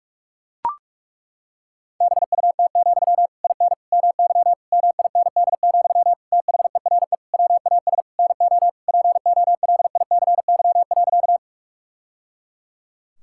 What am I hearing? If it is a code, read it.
Morse code, heard as a keyed tone, about 41 wpm, 700 Hz. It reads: BUT, IN MY MIND, THERE WAS NO POLICY.
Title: Sentences from Top 1000 Words - Review Part 5 - 45wpm